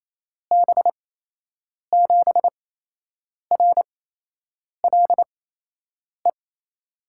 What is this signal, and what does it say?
Morse code, heard as a keyed tone, about 28 wpm, spaced as for 8 wpm, 700 Hz. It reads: B7RLE
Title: USA Call Signs - 6 Characters - 8x Character Spacing - 28wpm